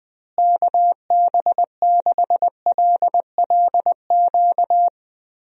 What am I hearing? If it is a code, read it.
Morse code, heard as a keyed tone, about 20 wpm, 700 Hz. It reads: KB6LLQ
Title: USA Call Signs - 6 Characters - Repeated 3x - 20wpm